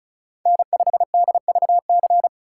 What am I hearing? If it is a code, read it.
Morse code, heard as a keyed tone, about 34 wpm, 700 Hz. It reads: N5DVC
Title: USA Call Signs - 5 Characters - 2x Character Spacing - 35wpm